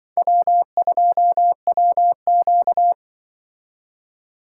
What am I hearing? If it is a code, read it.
Morse code, heard as a keyed tone, about 24 wpm, 700 Hz. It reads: W2WQ
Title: Speed Racing - USA 1x2 Call Signs 20wpm